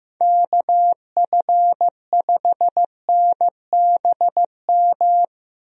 Morse code, heard as a keyed tone, about 15 wpm, 700 Hz. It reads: KF5NBM
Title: USA Call Signs - 6 Characters - Repeated 2x - 15wpm